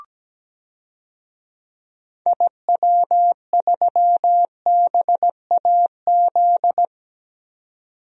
Morse code, heard as a keyed tone, about 17 wpm, 700 Hz. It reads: IW3BAZ
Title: DXCC-100 Call Signs - 6 Characters - 17wpm